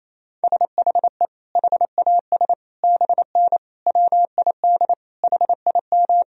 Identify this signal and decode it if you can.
Morse code, heard as a keyed tone, about 28 wpm, 700 Hz. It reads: SHE HAS BN WID HIM
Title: Sentences from Top 200 Words and Abbreviations 28wpm